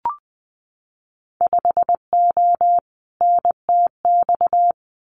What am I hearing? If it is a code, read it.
Morse code, heard as a keyed tone, about 20 wpm, 700 Hz. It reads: 5O NTX
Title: Winter Field Day Exchanges Encore 20wpm